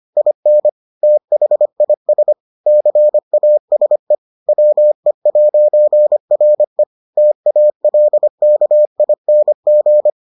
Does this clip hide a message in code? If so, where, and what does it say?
Yes: Morse code from 0.2 to 10.1 s, heard as a keyed tone, about 25 wpm, 600 Hz. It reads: IN THIS CASE WE'RE TALKING